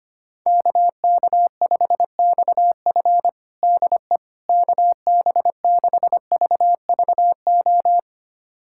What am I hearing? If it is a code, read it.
Morse code, heard as a keyed tone, about 25 wpm, 700 Hz. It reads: KK5XF DE KB6VVO
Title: USA Callsign DE Callsign 25wpm